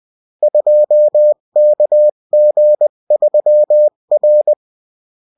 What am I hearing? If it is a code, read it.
Morse code, heard as a keyed tone, about 20 wpm, 600 Hz. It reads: 2KG3R
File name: morse mp3 / revision9-vitesse 18 mots